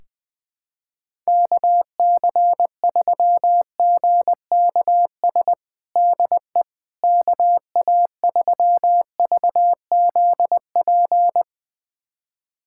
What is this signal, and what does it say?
Morse code, heard as a keyed tone, about 20 wpm, 700 Hz. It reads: KC3GKS DE KA3VZP
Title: USA Callsign DE Callsign - 6 Characters - 20wpm